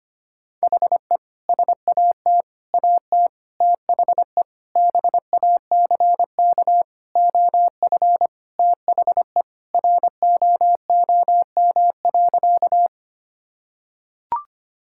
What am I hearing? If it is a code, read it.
Morse code, heard as a keyed tone, about 25 wpm, 700 Hz. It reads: HE SAT AT THE BACK OF THE ROOM.
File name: practice / 25wpm / Sentences with 3-Word Phrases - 25wpm